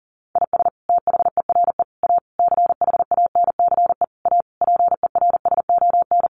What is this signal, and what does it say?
Morse code, heard as a keyed tone, about 40 wpm, 700 Hz. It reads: IS THERE A CHANCE A PERSON
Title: Sentences from Top 900 Words 40wpm